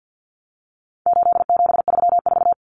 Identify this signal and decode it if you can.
Morse code, heard as a keyed tone, about 50 wpm, 700 Hz. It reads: 8734
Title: Common Cut Numbers - 4-Digit 50wpm